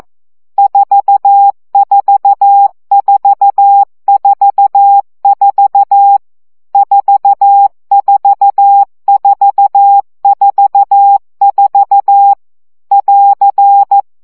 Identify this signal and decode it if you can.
Morse code, heard as a keyed tone, about 14 wpm, 800 Hz. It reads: 44444 44444 <AR>
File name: CW gyakorlas 4 12 baud